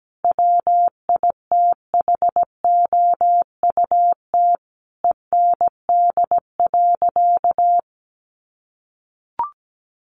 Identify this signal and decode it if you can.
Morse code, heard as a keyed tone, about 17 wpm, 700 Hz. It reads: WITHOUT END.
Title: Sentences from Top 600 Words 17wpm